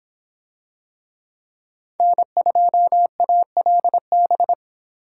Morse code, heard as a keyed tone, about 26 wpm, 700 Hz. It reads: N2ALB